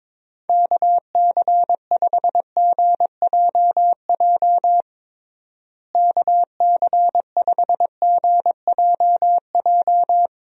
Morse code, heard as a keyed tone, about 22 wpm, 700 Hz. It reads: KC5GJJ KC5GJJ
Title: USA Call Signs - 6 Characters - Repeated 2x - 22wpm